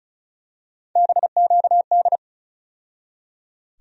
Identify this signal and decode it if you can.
Morse code, heard as a keyed tone, about 36 wpm, 700 Hz. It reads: BQD
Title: USA Call Sign Postfixes - 3 Letters 35wpm